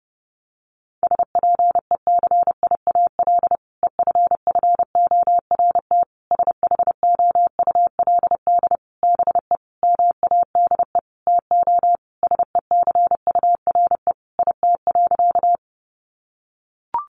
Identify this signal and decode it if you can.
Morse code, heard as a keyed tone, about 30 wpm, 700 Hz. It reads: SPECIAL EFFORT SHOULD BE MADE TO SECURE IT.
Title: Sentences from Top 1200 Words 30wpm